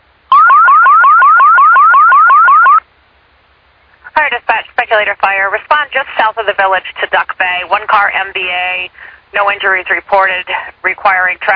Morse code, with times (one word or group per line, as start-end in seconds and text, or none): none